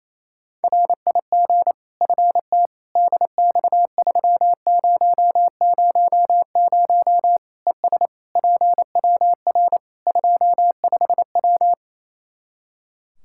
Morse code, heard as a keyed tone, about 27 wpm, 700 Hz. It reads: RIG FT DX3000 ES PWR 25W
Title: K1USN SST QSO Practice 28wpm